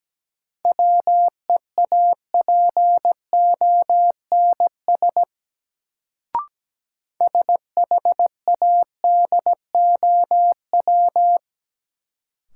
Morse code, heard as a keyed tone, about 17 wpm, 700 Hz. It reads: WEAPONS SHADOW
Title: Sets of 5 Words Encore - New Word Drill - 17wpm